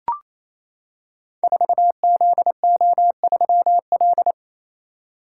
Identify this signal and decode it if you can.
Morse code, heard as a keyed tone, about 27 wpm, 700 Hz. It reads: 4ZO3L
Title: Code Groups - 5 Letter:Numbers - 28wpm